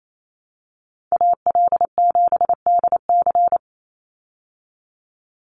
Morse code, heard as a keyed tone, about 27 wpm, 700 Hz. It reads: AL7DC
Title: USA Call Signs 28wpm